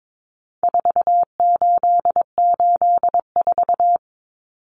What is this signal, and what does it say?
Morse code, heard as a keyed tone, about 22 wpm, 700 Hz. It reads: 4884
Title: Common Cut Numbers - 4-Digit 22wpm